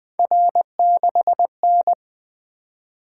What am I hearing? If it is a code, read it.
Morse code, heard as a keyed tone, about 24 wpm, 700 Hz. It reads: R6N
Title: Speed-Racing - Three Letter-Number Combinations - 20wpm